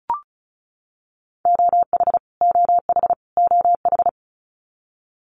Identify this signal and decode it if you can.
Morse code, heard as a keyed tone, about 34 wpm, 700 Hz. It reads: OH OH OH
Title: US States Repeated 3x 35wpm